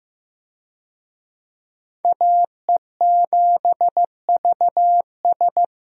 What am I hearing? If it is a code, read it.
Morse code, heard as a keyed tone, about 15 wpm, 700 Hz. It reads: AE7VS
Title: USA Call Signs - 5 Characters - Repeated 2x - 15wpm